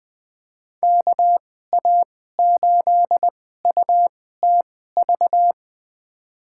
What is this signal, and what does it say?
Morse code, heard as a keyed tone, about 20 wpm, spaced as for 14 wpm, 700 Hz. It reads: KA8UTV
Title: USA Call Signs - 6 Characters - 2x Character Spacing - 20wpm